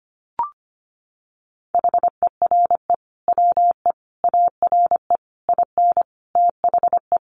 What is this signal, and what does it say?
Morse code, heard as a keyed tone, about 25 wpm, 700 Hz. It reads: HERE WE ARE IN THE
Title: Sentences from Top 1400 Words 25wpm